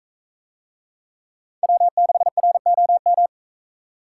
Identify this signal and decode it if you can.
Morse code, heard as a keyed tone, about 40 wpm, 700 Hz. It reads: W6ROM